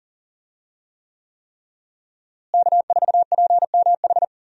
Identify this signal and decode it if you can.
Morse code, heard as a keyed tone, about 40 wpm, 700 Hz. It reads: K4PMH
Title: USA Call Signs - 5 Characters - Repeated 3x - 40wpm